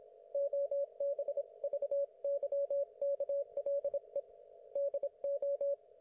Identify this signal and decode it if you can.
Morse code, heard as a keyed tone, about 26 wpm, 560 Hz. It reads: OBVYKLE DO
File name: Archiv / CUC194 / ok1fhi cul194